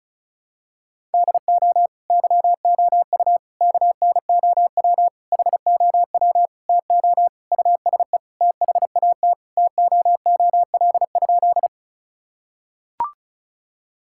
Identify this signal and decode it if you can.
Morse code, heard as a keyed tone, about 36 wpm, 700 Hz. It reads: DO YOU KNOW HOW TO USE THAT TOOL?
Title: Sentences from Top 1000 Words - Review Part 6 - 35wpm